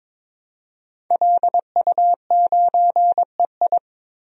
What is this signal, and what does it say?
Morse code, heard as a keyed tone, about 22 wpm, 700 Hz. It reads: LU9EI
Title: DXCC-100 Call Signs - 5 Characters - 22wpm